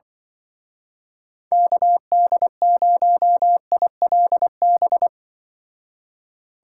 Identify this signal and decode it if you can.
Morse code, heard as a keyed tone, about 23 wpm, 700 Hz. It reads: KD0ILB
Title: Speed Racing - USA 2x3 Call Signs 20wpm